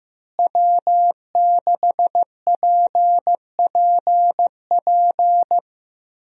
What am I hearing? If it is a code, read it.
Morse code, heard as a keyed tone, about 15 wpm, 700 Hz. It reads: W6PPP
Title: USA Call Signs - 5 Characters - Repeated 2x - 15wpm